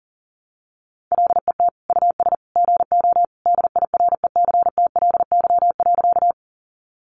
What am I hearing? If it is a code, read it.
Morse code, heard as a keyed tone, about 40 wpm, 700 Hz. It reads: LET US GO DIRECTLY.
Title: Sentences from Top 800 Words 40wpm